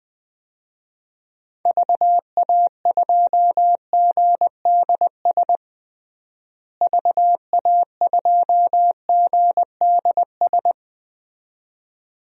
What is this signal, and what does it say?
Morse code, heard as a keyed tone, about 21 wpm, 700 Hz. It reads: VA2GDS VA2GDS
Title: DXCC-100 Call Signs - 6 Characters - Repeated 2x - 20wpm